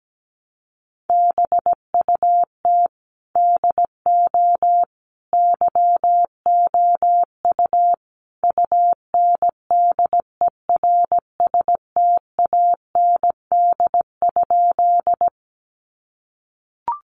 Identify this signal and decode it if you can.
Morse code, heard as a keyed tone, about 17 wpm, 700 Hz. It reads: BUT DO YOU UNDERSTAND?
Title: Sentences from Top 600 Words 17wpm